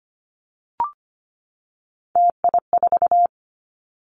Morse code, heard as a keyed tone, about 25 wpm, 700 Hz. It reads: TI4
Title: Callsign Prefixes for Top 100 DXCC - 25wpm